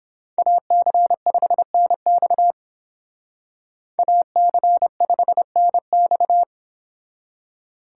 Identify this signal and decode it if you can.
Morse code, heard as a keyed tone, about 31 wpm, 700 Hz. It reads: AC5NX AC5NX